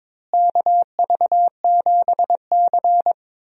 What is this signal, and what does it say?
Morse code, heard as a keyed tone, about 22 wpm, 700 Hz. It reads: KV7C
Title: USA Call Signs - 4 Characters - Repeated 3x - 22wpm